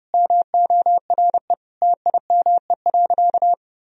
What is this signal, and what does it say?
Morse code, heard as a keyed tone, about 30 wpm, 700 Hz. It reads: MORE TIME.
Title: Sentences from Top 1000 Words - Review Part 3 - 30wpm